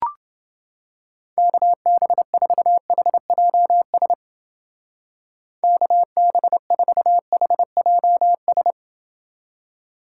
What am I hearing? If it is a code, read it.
Morse code, heard as a keyed tone, about 30 wpm, 700 Hz. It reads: KB4HJS KB4HJS